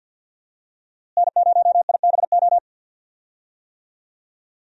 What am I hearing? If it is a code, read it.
Morse code, heard as a keyed tone, about 51 wpm, 700 Hz. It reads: N0IBO
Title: USA Call Signs - 5 Characters - Repeated 3x - 50wpm